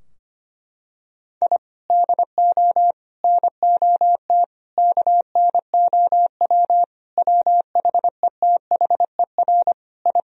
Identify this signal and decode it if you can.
Morse code, heard as a keyed tone, about 25 wpm, 700 Hz. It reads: I DO NOT KNOW WHETHER I
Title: Sentences from Top 1000 Words - Review Part 3 - 25wpm